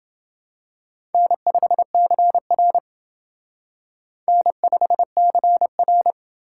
Morse code, heard as a keyed tone, about 28 wpm, 700 Hz. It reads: N5CR N5CR